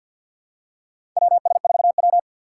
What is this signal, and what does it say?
Morse code, heard as a keyed tone, about 49 wpm, 700 Hz. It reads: WS4W
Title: USA Call Signs - 2x Character Spacing - 50wpm